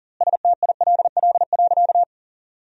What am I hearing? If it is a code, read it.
Morse code, heard as a keyed tone, about 40 wpm, 700 Hz. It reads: STILL.
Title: 4x Word-Spacing - Sentences from Top 400 Words - 40wpm